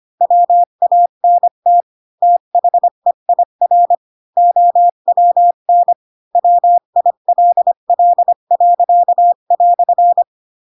Morse code, heard as a keyed tone, about 25 wpm, 700 Hz. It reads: WANT THEIR OWN WILL."
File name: Pinocchio / Pinocchio ch8 20 25 ik1hge